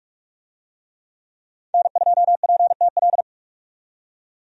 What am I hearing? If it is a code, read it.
Morse code, heard as a keyed tone, about 46 wpm, 700 Hz. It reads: N2PTL
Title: USA Call Signs - 5 Characters - 45wpm